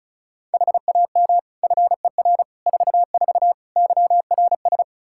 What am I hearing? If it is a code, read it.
Morse code, heard as a keyed tone, about 36 wpm, 700 Hz. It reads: HAM FER 44 YRS
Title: K1USN SST QSO Practice 35wpm